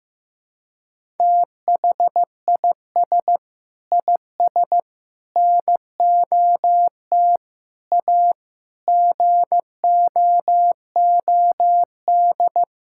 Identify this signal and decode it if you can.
Morse code, heard as a keyed tone, about 15 wpm, 700 Hz. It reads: THIS IS NOT A GOOD
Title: Sentences from Top 1000 Words - Review Part 6 - 15wpm